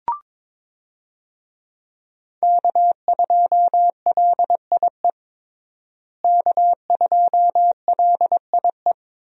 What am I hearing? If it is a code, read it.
Morse code, heard as a keyed tone, about 23 wpm, 700 Hz. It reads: K2LIE K2LIE
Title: USA Call Signs - 5 Characters - Repeated 2x - 22wpm